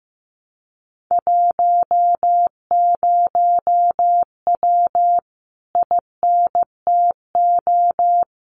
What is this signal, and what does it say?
Morse code, heard as a keyed tone, about 15 wpm, 700 Hz. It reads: 10W INTO
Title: Common QSO Phrases Encore 15wpm